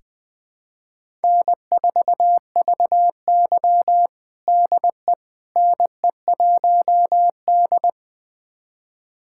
Morse code, heard as a keyed tone, about 20 wpm, 700 Hz. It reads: N4VY DE NE1D
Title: USA Callsign DE Callsign - 4 Characters - 20wpm